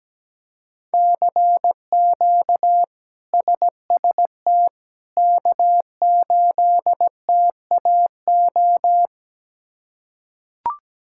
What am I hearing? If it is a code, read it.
Morse code, heard as a keyed tone, about 17 wpm, 700 Hz. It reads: CQ SST K8TAO
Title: K1USN SST QSO Practice - 17wpm